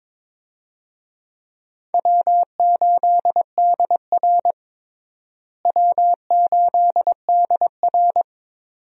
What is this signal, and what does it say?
Morse code, heard as a keyed tone, about 22 wpm, 700 Hz. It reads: W8DR W8DR
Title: USA Call Signs - 4 Characters - Repeated 2x - 22wpm